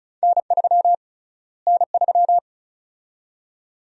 Morse code, heard as a keyed tone, about 34 wpm, 700 Hz. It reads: N3 N3
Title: USA Call Sign Prefixes - Repeated 2x - 35wpm